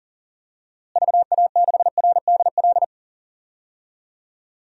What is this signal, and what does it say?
Morse code, heard as a keyed tone, about 40 wpm, 700 Hz. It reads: VA6RDL
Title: DXCC-100 Call Signs - 6 Characters - Repeated 3x - 40wpm